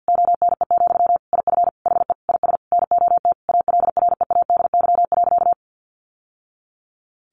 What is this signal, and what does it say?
Morse code, heard as a keyed tone, about 50 wpm, 700 Hz. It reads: ONE, IF HE IS NOT ALREADY.
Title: Sentences from Top 900 Words 50wpm